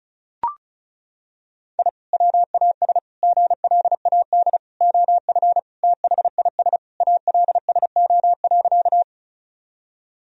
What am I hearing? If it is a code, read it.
Morse code, heard as a keyed tone, about 34 wpm, 700 Hz. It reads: I WAS GLAD OF THIS ALSO.
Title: Sentences from Top 1000 Words - Review Part 1 - 35wpm